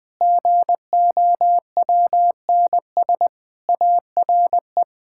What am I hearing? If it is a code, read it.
Morse code, heard as a keyed tone, about 21 wpm, 700 Hz. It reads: GOWNS ARE
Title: Pangram Challenge v2 - 20wpm